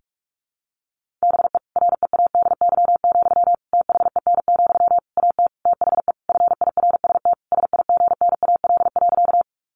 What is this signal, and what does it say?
Morse code, heard as a keyed tone, about 45 wpm, 700 Hz. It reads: BE READY, THEN, AT THE FIRST SIGNAL.
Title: Sentences from Top 1400 Words 45wpm